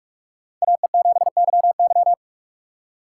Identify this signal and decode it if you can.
Morse code, heard as a keyed tone, about 46 wpm, 700 Hz. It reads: AE7YY
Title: USA Call Signs - 4x Character Spacing - 45wpm